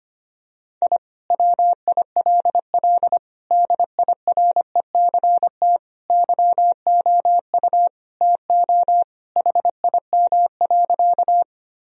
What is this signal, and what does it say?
Morse code, heard as a keyed tone, about 25 wpm, 700 Hz. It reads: I WILL DIRECT YOU TO HIM.
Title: Sentences from Top 1000 Words - Review Part 2 - 25wpm